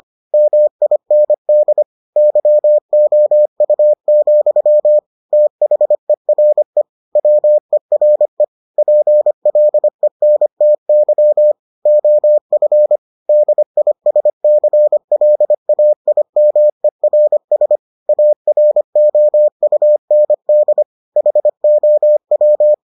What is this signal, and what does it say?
Morse code, heard as a keyed tone, about 25 wpm, 600 Hz. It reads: MIND YOU, THERE WERE PLENTY OF DISCLAIMERS AROUND HOW